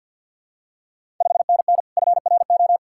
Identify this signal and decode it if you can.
Morse code, heard as a keyed tone, about 49 wpm, 700 Hz. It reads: 5NN FRO
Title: Ontario QSO Party Exchange 50wpm